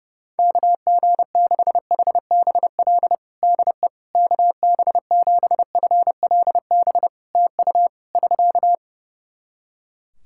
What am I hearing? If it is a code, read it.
Morse code, heard as a keyed tone, about 30 wpm, 700 Hz. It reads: KG6HBL DE KB7FLB TU <SK>
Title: Practice QSO - Exchange 4 - Ending - 30wpm